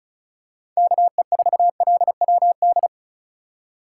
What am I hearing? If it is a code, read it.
Morse code, heard as a keyed tone, about 36 wpm, 700 Hz. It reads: KE4LWD
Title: USA Call Signs - 6 Characters - 3x Character Spacing - 35wpm